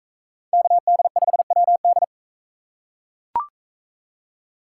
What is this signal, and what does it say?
Morse code, heard as a keyed tone, about 41 wpm, 700 Hz. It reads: KD5WD